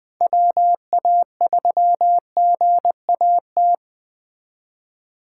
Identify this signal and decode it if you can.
Morse code, heard as a keyed tone, about 20 wpm, 700 Hz. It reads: WA3GAT